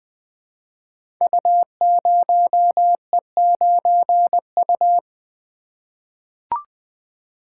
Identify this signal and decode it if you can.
Morse code, heard as a keyed tone, about 20 wpm, 700 Hz. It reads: U0E9U
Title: Code Groups - 5 Letter:Numbers - 20wpm